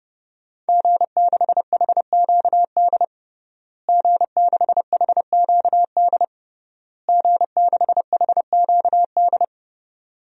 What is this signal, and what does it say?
Morse code, heard as a keyed tone, about 30 wpm, 700 Hz. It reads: G6HQD G6HQD G6HQD
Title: DXCC-100 Call Signs - 5 Characters - Repeated 3x - 30wpm